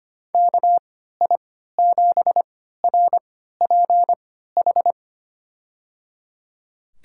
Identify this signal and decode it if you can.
Morse code, heard as a keyed tone, about 25 wpm, spaced as for 14 wpm, 700 Hz. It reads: KI7RPH
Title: USA Call Signs - 3x Character Spacing - 25wpm